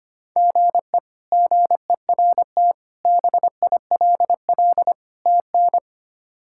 Morse code, heard as a keyed tone, about 25 wpm, 700 Hz. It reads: GE GERT BILL TN
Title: K1USN SST QSO Practice - 25wpm